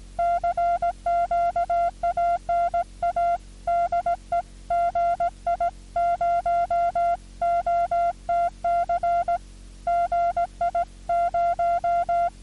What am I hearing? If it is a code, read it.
Morse code, heard as a keyed tone, about 18 wpm, 1380 Hz, with noise. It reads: CQ ANA DE GI0OTC GI0
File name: 6M audio / W1JJM AU GI0OTC 27Dec2000 1618z